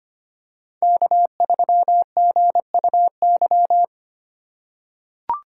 Morse code, heard as a keyed tone, about 25 wpm, 700 Hz. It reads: K3GUY
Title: USA Call Signs - 6x Character Spacing - 25wpm